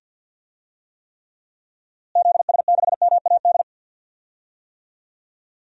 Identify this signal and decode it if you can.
Morse code, heard as a keyed tone, about 53 wpm, 700 Hz. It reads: ZS6MAD
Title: DXCC-100 Call Signs - 6 Characters - Repeated 2x - 50wpm